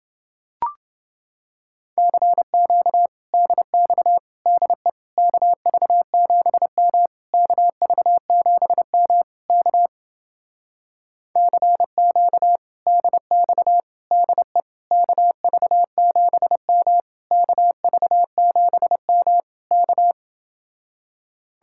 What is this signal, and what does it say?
Morse code, heard as a keyed tone, about 28 wpm, 700 Hz. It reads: CQ DX DE KV7M KV7M K CQ DX DE KV7M KV7M K